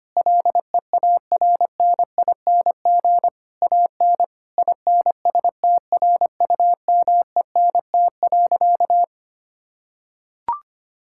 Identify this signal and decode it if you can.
Morse code, heard as a keyed tone, about 25 wpm, 700 Hz. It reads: LEARNING AN INSTRUMENT.
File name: practice / 25wpm / Difficult Sentences with Hints - 25wpm